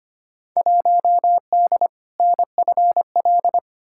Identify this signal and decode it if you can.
Morse code, heard as a keyed tone, about 25 wpm, 700 Hz. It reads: 1D NFL
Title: Field Day Exchanges - 25wpm